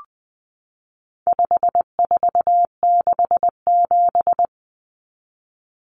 Morse code, heard as a keyed tone, about 20 wpm, 700 Hz. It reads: 5467
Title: Common Cut Numbers - 4-Digit 20wpm